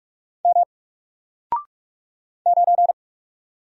Morse code, heard as a keyed tone, about 44 wpm, 700 Hz. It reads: M 9
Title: Single Letter-Number Alt 45wpm